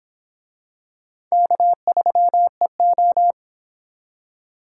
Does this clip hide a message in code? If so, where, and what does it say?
Yes: Morse code from 1.3 to 3.3 s, heard as a keyed tone, about 24 wpm, 700 Hz. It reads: K3EO